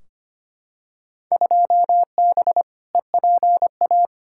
Morse code, heard as a keyed tone, about 25 wpm, 700 Hz. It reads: 2B EPA
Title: Field Day Exchanges Encore - 25wpm